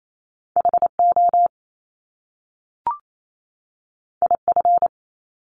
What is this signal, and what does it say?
Morse code, heard as a keyed tone, about 28 wpm, 700 Hz. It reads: HO IF
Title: Top Two Letter Combinations in Words 28wpm